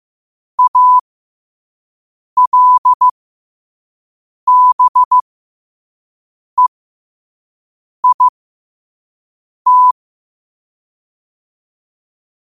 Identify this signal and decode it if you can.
Morse code, heard as a keyed tone, about 15 wpm, spaced as for 5 wpm, 1000 Hz. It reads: ALBEIT
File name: qotdfiles / 5wpm